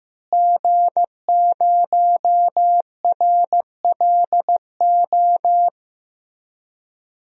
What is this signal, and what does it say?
Morse code, heard as a keyed tone, about 15 wpm, 700 Hz. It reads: G0RLO
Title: DXCC-100 Call Signs - 5 Characters - Repeated 3x - 15wpm